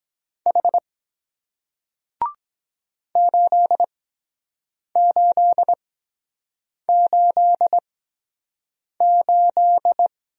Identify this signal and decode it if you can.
Morse code, heard as a keyed tone, about 23 wpm, 700 Hz. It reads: H 8 8 8 8